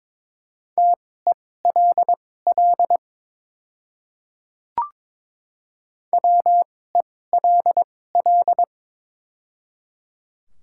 Morse code, heard as a keyed tone, about 22 wpm, spaced as for 16 wpm, 700 Hz. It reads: TELL WELL
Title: 2x Character Spacing - Top 200 Words - 22wpm